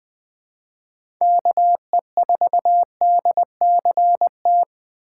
Morse code, heard as a keyed tone, about 21 wpm, 700 Hz. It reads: KE4DCT
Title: USA Call Signs - 2x Character Spacing - 20wpm